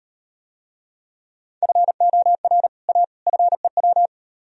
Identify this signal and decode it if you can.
Morse code, heard as a keyed tone, about 37 wpm, 700 Hz. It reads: FOR A FEW